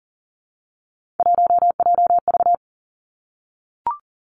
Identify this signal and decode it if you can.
Morse code, heard as a keyed tone, about 40 wpm, 700 Hz. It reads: 1JV
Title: Three Challenging Letter-Number Combinations 40wpm